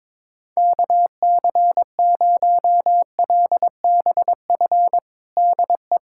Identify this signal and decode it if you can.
Morse code, heard as a keyed tone, about 21 wpm, 700 Hz. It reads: KC0LBF DE
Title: USA Callsign DE Callsign - 6 Characters - 22wpm